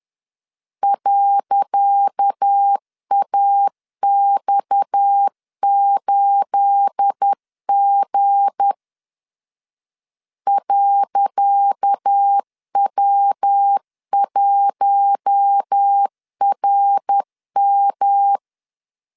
Morse code, heard as a keyed tone, about 10 wpm, 800 Hz. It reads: .AX8G .W1RM
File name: LEZIONI / 10 SIGNS / 10C DOT 10wpm 2940